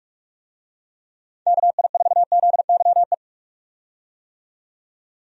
Morse code, heard as a keyed tone, about 46 wpm, 700 Hz. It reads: KI4ZYE
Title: USA Call Signs - 6 Characters - 45wpm